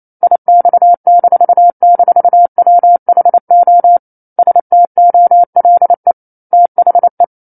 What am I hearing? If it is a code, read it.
Morse code, heard as a keyed tone, about 28 wpm, 700 Hz. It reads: IX--WHO STOLE THE
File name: 09-Chapter-09 28 WPM